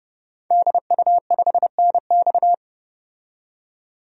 Morse code, heard as a keyed tone, about 29 wpm, 700 Hz. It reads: DU5NX
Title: DXCC-100 Call Signs - 5 Characters - 30wpm